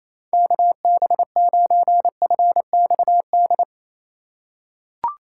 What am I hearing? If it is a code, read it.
Morse code, heard as a keyed tone, about 28 wpm, 700 Hz. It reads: KB9FXD
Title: USA Call Signs - 6 Characters - 2x Character Spacing - 28wpm